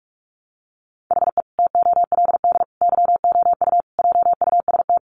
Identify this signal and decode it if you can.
Morse code, heard as a keyed tone, about 44 wpm, 700 Hz. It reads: HE TOLD YOU JUST
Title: Sentences from Top 900 Words 45wpm